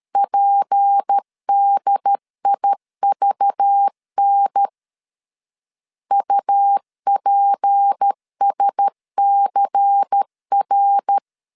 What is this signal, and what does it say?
Morse code, heard as a keyed tone, about 13 wpm, 800 Hz. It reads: PDIVN UPSCR
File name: LEZIONI / 6 PKX0 / 6 P 12wpm 2033